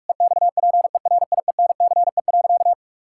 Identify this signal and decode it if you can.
Morse code, heard as a keyed tone, about 46 wpm, 700 Hz. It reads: EXPERIENCE.
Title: Sentences from Top 1000 Words - Review Part 5 - 45wpm